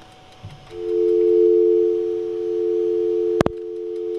Dial tone (North America), 350 + 440 Hz.